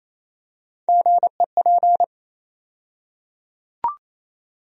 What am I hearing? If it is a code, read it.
Morse code, heard as a keyed tone, about 27 wpm, 700 Hz. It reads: GEP
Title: USA Call Sign Postfixes - 3 Letters 28wpm